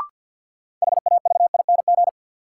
Morse code, heard as a keyed tone, about 47 wpm, 700 Hz. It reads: HAVING
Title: Sets of 5 Words Encore - New Word Drill - 50wpm